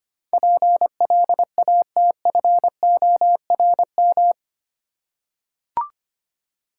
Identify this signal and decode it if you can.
Morse code, heard as a keyed tone, about 25 wpm, 700 Hz. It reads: PLATFORM
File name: Eight Letter Words - 25wpm